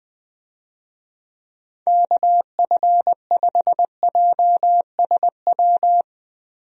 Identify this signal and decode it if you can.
Morse code, heard as a keyed tone, about 20 wpm, 700 Hz. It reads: KF5JSW
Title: USA Call Signs - 6 Characters - Repeated 3x - 20wpm